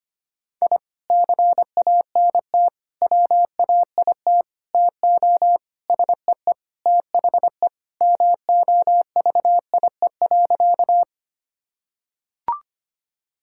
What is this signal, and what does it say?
Morse code, heard as a keyed tone, about 25 wpm, 700 Hz. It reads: I CANT WAIT TO SEE THE MOVIE.
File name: practice / 25wpm / Sentences with 3-Word Phrases - 25wpm